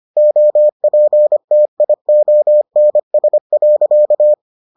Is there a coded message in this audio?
Yes: Morse code from 0.2 to 4.3 s, heard as a keyed tone, about 25 wpm, 600 Hz. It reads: OPTIONS.